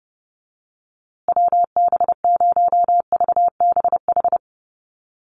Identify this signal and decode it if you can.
Morse code, heard as a keyed tone, about 30 wpm, 700 Hz. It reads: WB0VBH
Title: USA Call Signs Repeated 3x 30wpm